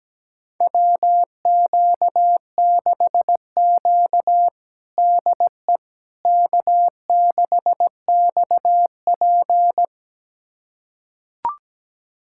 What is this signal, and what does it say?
Morse code, heard as a keyed tone, about 17 wpm, 700 Hz. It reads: WQ6Q DE K6XP
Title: USA Callsign DE Callsign - 4 Characters - 17wpm